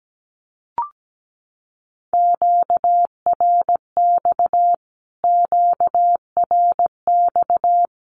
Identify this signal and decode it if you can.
Morse code, heard as a keyed tone, about 17 wpm, 700 Hz. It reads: QRX QRX
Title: Q-Codes Repeated 3x 17wpm